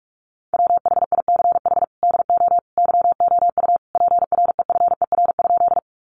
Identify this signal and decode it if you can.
Morse code, heard as a keyed tone, about 44 wpm, 700 Hz. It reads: WHICH DO YOU PREFER?
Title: Sentences from Top 1100 Words 45wpm